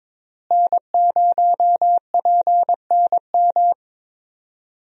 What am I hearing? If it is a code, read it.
Morse code, heard as a keyed tone, about 21 wpm, 700 Hz. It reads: N0PNM
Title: USA Call Signs - 5 Characters - Repeated 2x - 22wpm